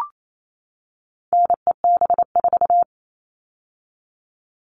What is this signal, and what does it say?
Morse code, heard as a keyed tone, about 28 wpm, 700 Hz. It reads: NEB4
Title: All Cut Numbers - 4-Digit 28wpm